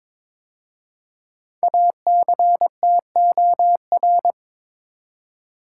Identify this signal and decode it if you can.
Morse code, heard as a keyed tone, about 22 wpm, 700 Hz. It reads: ACTOR
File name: 5 Letter Words - First 3 Letters Given - 22wpm